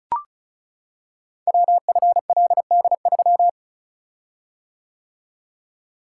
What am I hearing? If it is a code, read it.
Morse code, heard as a keyed tone, about 36 wpm, 700 Hz. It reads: WFLD3
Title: Code Groups - 5 Letter:Numbers - 35wpm